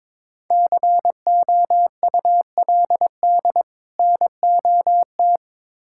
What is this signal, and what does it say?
Morse code, heard as a keyed tone, about 22 wpm, 700 Hz. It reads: COULD NOT
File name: Build A Phrase - 2-Word Phrases - 22wpm